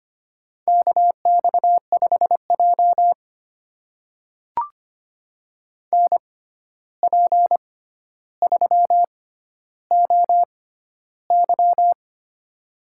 Morse code, heard as a keyed tone, about 25 wpm, spaced as for 12 wpm, 700 Hz. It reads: KX5J NP3OY
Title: USA Call Signs - 6x Character Spacing - 25wpm